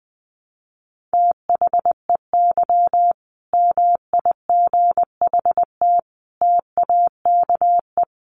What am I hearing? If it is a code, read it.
Morse code, heard as a keyed tone, about 19 wpm, 700 Hz. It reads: THEY MIGHT TAKE
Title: Sentences from Top 1000 Words 20wpm